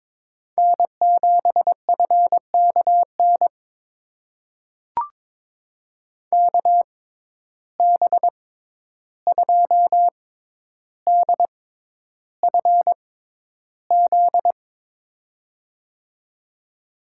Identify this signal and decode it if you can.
Morse code, heard as a keyed tone, about 23 wpm, spaced as for 11 wpm, 700 Hz. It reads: N7FKN KB2DFZ
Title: USA Call Signs - 6x Character Spacing - 22wpm